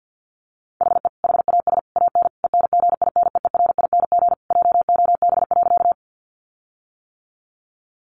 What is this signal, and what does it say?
Morse code, heard as a keyed tone, about 50 wpm, 700 Hz. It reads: HE HAS AN ENGINEERING JOB.
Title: Sentences from Top 1500 Words 50wpm